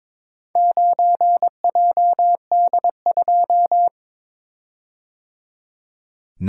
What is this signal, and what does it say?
Morse code, heard as a keyed tone, about 22 wpm, 700 Hz. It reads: 9JD2
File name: practice / 22wpm / Four Letter-Number Combinations - 22wpm